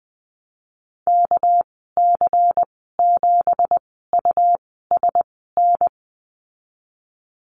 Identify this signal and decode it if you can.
Morse code, heard as a keyed tone, about 20 wpm, spaced as for 14 wpm, 700 Hz. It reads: KC7USN
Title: 245 Lesson 39g - Letter z - Call Signs 20wpm